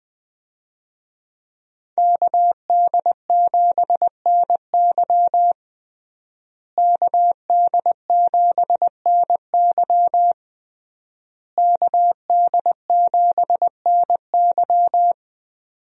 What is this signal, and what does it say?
Morse code, heard as a keyed tone, about 21 wpm, 700 Hz. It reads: KD7NY KD7NY KD7NY
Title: USA Call Signs - 5 Characters - Repeated 3x - 20wpm